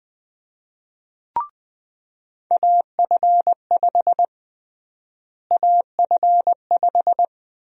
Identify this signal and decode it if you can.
Morse code, heard as a keyed tone, about 21 wpm, 700 Hz. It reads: AF5 AF5
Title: USA Call Sign Prefixes - Repeated 2x - 20wpm